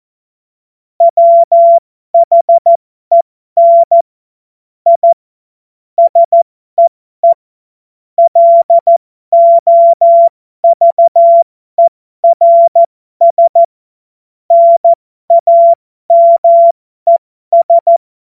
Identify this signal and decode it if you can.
Morse code, heard as a keyed tone, about 14 wpm, spaced as for 12 wpm, 680 Hz. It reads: WHEN I SEE LOVERS NAMES
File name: Morsekurs / Witze